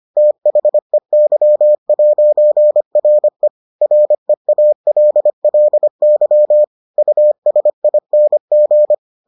Morse code, heard as a keyed tone, about 25 wpm, 600 Hz. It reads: THEY'RE REALLY USING